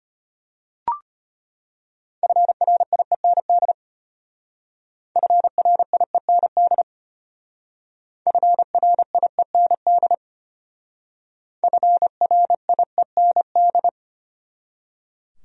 Morse code, heard as a keyed tone, about 33 wpm, 700 Hz. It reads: FRIEND FRIEND FRIEND FRIEND